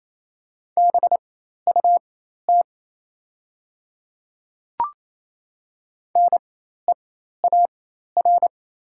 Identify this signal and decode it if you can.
Morse code, heard as a keyed tone, about 27 wpm, spaced as for 13 wpm, 700 Hz. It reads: BUT NEAR
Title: 4x Character Spacing - Top 200 Words - 28wpm